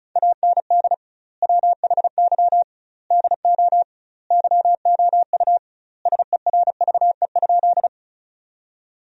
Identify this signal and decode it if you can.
Morse code, heard as a keyed tone, about 36 wpm, 700 Hz. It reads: AND WHY DO YOU SERVE?
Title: 2x Word-Spacing - Sentences from Top 400 Words - 35wpm